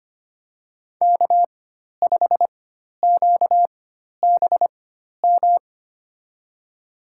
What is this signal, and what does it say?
Morse code, heard as a keyed tone, about 25 wpm, spaced as for 12 wpm, 700 Hz. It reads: K5QBM
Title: USA Call Signs - 5 Characters - 4x Character Spacing - 25wpm